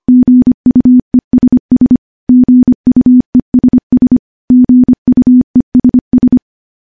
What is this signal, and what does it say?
Morse code, heard as a keyed tone, about 25 wpm, 250 Hz. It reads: GUESS GUESS GUESS
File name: morse-training-25WPM